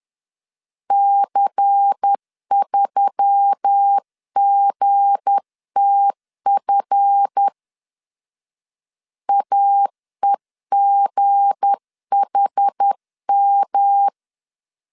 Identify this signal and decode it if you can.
Morse code, heard as a keyed tone, about 10 wpm, 800 Hz. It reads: C3GTF AEGHM
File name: LEZIONI / 7 GF9Z / 7 GF 10wpm 2514